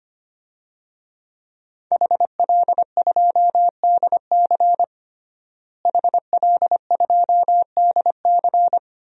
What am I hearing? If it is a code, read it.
Morse code, heard as a keyed tone, about 25 wpm, 700 Hz. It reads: HL2DC HL2DC
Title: DXCC-100 Call Signs - 5 Characters - Repeated 2x - 25wpm